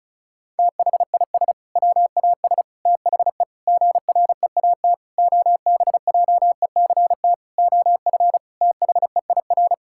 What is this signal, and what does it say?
Morse code, heard as a keyed tone, about 36 wpm, 700 Hz. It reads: THIS WAS THE GREAT OBJECT OF THEIR
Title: Sentences from Top 1000 Words - Review Part 5 - 35wpm